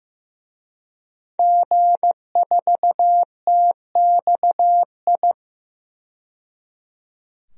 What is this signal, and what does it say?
Morse code, heard as a keyed tone, about 15 wpm, 700 Hz. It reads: G4TXI
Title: DXCC-100 Call Signs - 5 Characters - 15wpm